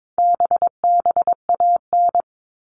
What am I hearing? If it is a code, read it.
Morse code, heard as a keyed tone, about 22 wpm, 700 Hz. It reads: BBAN
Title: All Cut Numbers - 4-Digit 22wpm